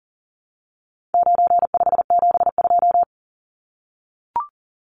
Morse code, heard as a keyed tone, about 40 wpm, 700 Hz. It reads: 9572
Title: Numbers - 4-Digit 40wpm